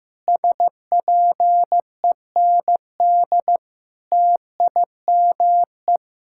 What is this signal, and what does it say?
Morse code, heard as a keyed tone, about 15 wpm, 700 Hz. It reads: SPEND TIME
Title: Sentences from Top 1000 Words - Review Part 6 - 15wpm